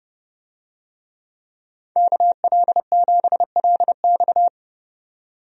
Morse code, heard as a keyed tone, about 30 wpm, 700 Hz. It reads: KL7LX
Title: DXCC-100 Call Signs - 5 Characters - 30wpm